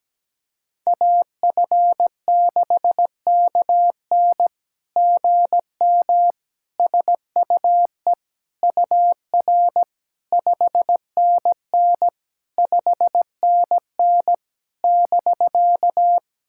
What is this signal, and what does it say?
Morse code, heard as a keyed tone, about 17 wpm, 700 Hz. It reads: AF6KN GM SUE UR 5NN 5NN <BK>
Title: SOTA QSO Practice 17wpm